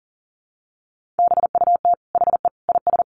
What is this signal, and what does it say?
Morse code, heard as a keyed tone, about 40 wpm, 700 Hz. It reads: BUT HE IS
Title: Sentences from Top 1400 Words 40wpm